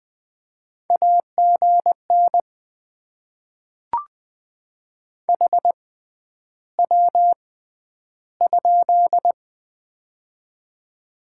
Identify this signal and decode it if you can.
Morse code, heard as a keyed tone, about 20 wpm, spaced as for 10 wpm, 700 Hz. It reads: AGN HW?
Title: Top 90 QSO Elements - 6x Character Spacing - 20wpm